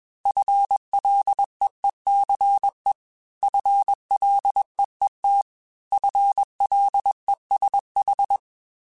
Morse code, heard as a keyed tone, about 21 wpm, 800 Hz. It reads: FLEECE FLEET FLESH
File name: FL Runs / FL Runs mp3s / FL Runs @20